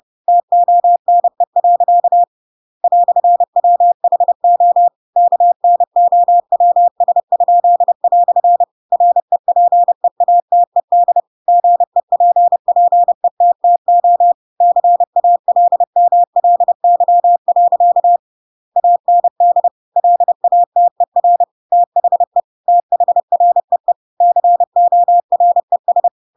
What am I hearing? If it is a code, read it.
Morse code, heard as a keyed tone, about 30 wpm, 700 Hz. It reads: TONE. "WHO KNOWS?" REPEATED GEPPETTO CALMLY. AND LATER THE THREE CORES
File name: Pinocchio / Pinocchio ch7 30 30 ik1hge